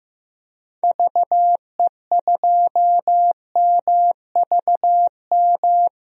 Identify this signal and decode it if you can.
Morse code, heard as a keyed tone, about 15 wpm, 700 Hz. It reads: VE2MVM
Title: DXCC-100 Call Signs - 6 Characters - Repeated 2x - 15wpm